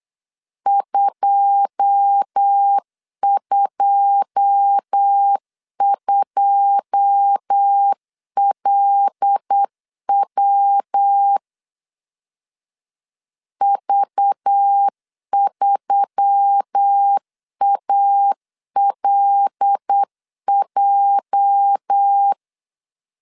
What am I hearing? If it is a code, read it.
Morse code, heard as a keyed tone, about 8 wpm, 800 Hz. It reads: 222LW V3ALJ
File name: CW / DET / 1 9 1500C S B 8wpm 5624